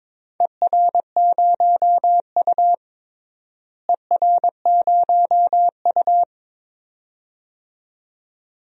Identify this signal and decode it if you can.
Morse code, heard as a keyed tone, about 22 wpm, 700 Hz. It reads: ER0U ER0U
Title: DXCC-100 Call Signs - 4 Characters - Repeated 2x - 22wpm